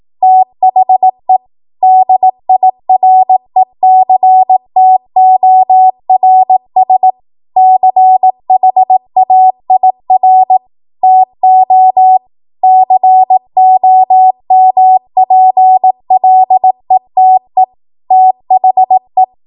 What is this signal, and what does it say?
Morse code, heard as a keyed tone, about 18 wpm, 750 Hz. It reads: THE DIRECTORS CHAIR TO COMPLETE THE
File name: Archive / 18 WPM / 18WPM